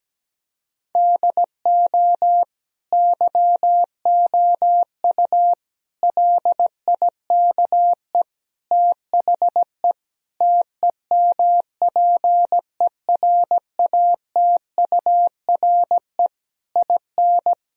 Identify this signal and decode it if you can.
Morse code, heard as a keyed tone, about 17 wpm, 700 Hz. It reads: DO YOU LIKE THE TEMPERATURE IN
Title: Sentences from Top 1000 Words - Review Part 4 - 17wpm